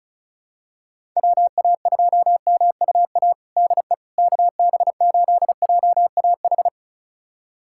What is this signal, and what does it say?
Morse code, heard as a keyed tone, about 36 wpm, 700 Hz. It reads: WA2MUA DE KB8JAH
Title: USA Callsign DE Callsign 35wpm